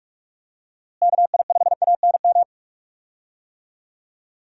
Morse code, heard as a keyed tone, about 46 wpm, 700 Hz. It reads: KI5ANM
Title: USA Call Signs - 6 Characters - Repeated 3x - 45wpm